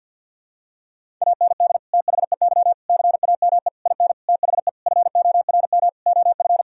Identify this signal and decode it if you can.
Morse code, heard as a keyed tone, about 52 wpm, 700 Hz. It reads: AND THEY CAME IN THE FORM OF
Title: Sentences from Top 1000 Words - Review Part 1 - 50wpm